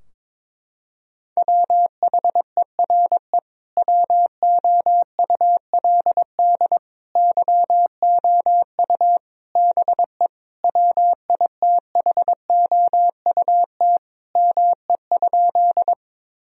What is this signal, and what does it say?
Morse code, heard as a keyed tone, about 22 wpm, 700 Hz. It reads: WHERE WOULD YOU BE WITHOUT ME?
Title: Sentences from Top 1000 Words - Review Part 3 - 22wpm